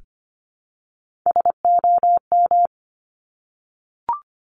Morse code, heard as a keyed tone, about 25 wpm, 700 Hz. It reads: SOM
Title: Top Three Letter Combinations in Words 25wpm